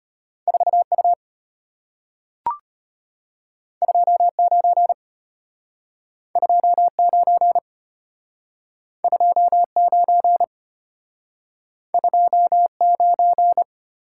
Morse code, heard as a keyed tone, about 33 wpm, 690 Hz. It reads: 4U 29 29 29 29